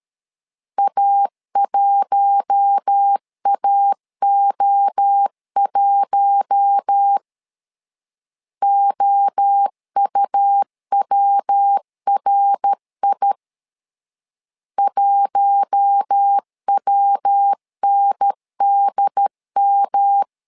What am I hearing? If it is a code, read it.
Morse code, heard as a keyed tone, about 13 wpm, 800 Hz. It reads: A1AO1 OUWRI 1WNDM
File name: LEZIONI / 4 Q1 12wpm 2028